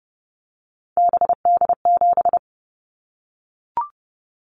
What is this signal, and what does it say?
Morse code, heard as a keyed tone, about 30 wpm, 700 Hz. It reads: BD7
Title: Callsign Prefixes for Top 100 DXCC - 30wpm